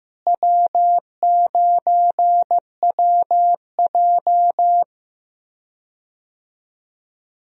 Morse code, heard as a keyed tone, about 16 wpm, 700 Hz. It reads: W9WJ
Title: USA Call Signs - 4 Characters - Repeated 3x - 15wpm